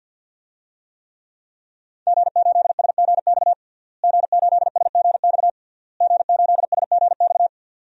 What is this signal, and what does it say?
Morse code, heard as a keyed tone, about 51 wpm, 700 Hz. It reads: G8SGX G8SGX G8SGX
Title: DXCC-100 Call Signs - 5 Characters - Repeated 3x - 50wpm